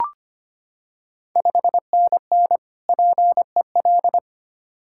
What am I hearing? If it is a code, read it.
Morse code, heard as a keyed tone, about 25 wpm, 700 Hz. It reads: 5NN PEL
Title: Ontario QSO Party Exchange 25wpm